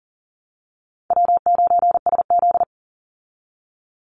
Morse code, heard as a keyed tone, about 40 wpm, 700 Hz. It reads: W9SZ
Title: Easy Call Signs Repeated 3x - 40wpm